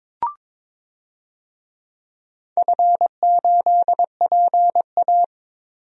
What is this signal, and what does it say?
Morse code, heard as a keyed tone, about 22 wpm, 700 Hz. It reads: F8PA
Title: DXCC-100 Call Signs - 4 Characters - 22wpm